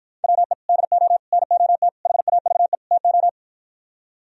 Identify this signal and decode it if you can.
Morse code, heard as a keyed tone, about 59 wpm, 700 Hz. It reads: WE DO NOT HAVE TO